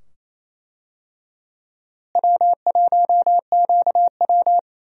Morse code, heard as a keyed tone, about 28 wpm, 700 Hz. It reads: W1QW
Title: USA Call Signs - 4 Characters - Repeated 2x - 28wpm